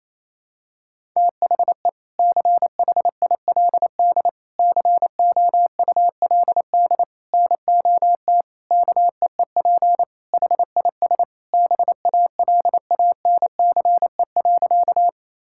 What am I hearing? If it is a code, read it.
Morse code, heard as a keyed tone, about 28 wpm, 700 Hz. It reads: THE CHILD COULD NOT KEEP HIS BALANCE.
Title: Sentences from Top 1000 Words - Review Part 6 - 28wpm